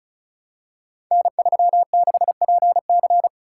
Morse code, heard as a keyed tone, about 36 wpm, 690 Hz. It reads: N36PC
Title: Code Groups - 5 Letter:Numbers - 35wpm